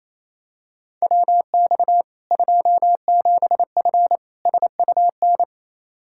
Morse code, heard as a keyed tone, about 28 wpm, 700 Hz. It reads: WX 27F SUN
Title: Practice QSO - Exchange 2 - Info - 28wpm